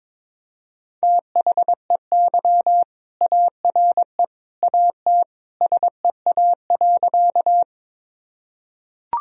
Morse code, heard as a keyed tone, about 22 wpm, 700 Hz. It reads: THEY ARE AT SEA.
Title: Sentences from Top 1000 Words - Review Part 1 - 22wpm